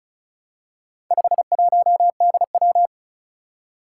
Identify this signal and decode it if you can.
Morse code, heard as a keyed tone, about 36 wpm, 690 Hz. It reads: 51DW
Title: Four Challenging Letter-Number Combinations 35wpm